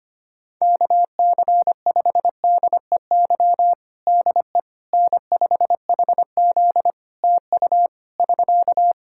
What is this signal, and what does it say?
Morse code, heard as a keyed tone, about 25 wpm, 700 Hz. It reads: KC5DEY DE N5HZ TU <SK>
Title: Practice QSO - Exchange 4 - Ending - 25wpm